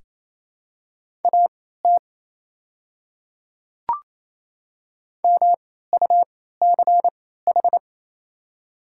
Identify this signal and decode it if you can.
Morse code, heard as a keyed tone, about 28 wpm, spaced as for 16 wpm, 700 Hz. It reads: AT MUCH
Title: 3x Character Spacing - Top 200 Words - 28wpm